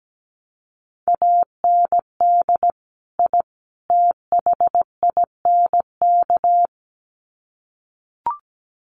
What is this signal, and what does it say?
Morse code, heard as a keyed tone, about 17 wpm, 700 Hz. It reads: AND I THINK
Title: Sets of 3 Words 17wpm